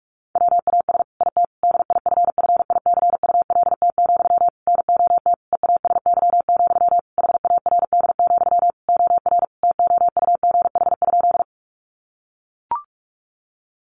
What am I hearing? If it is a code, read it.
Morse code, heard as a keyed tone, about 44 wpm, 700 Hz. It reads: WAS IT DIFFICULT, NOT EASY, HARD, OR TOUGH?
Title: Sentences from Top 1200 Words 45wpm